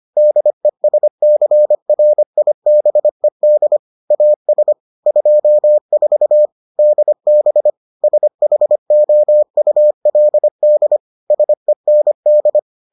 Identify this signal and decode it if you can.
Morse code, heard as a keyed tone, about 25 wpm, 600 Hz. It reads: DESCRIBED AS 24 DB SHOULD SEND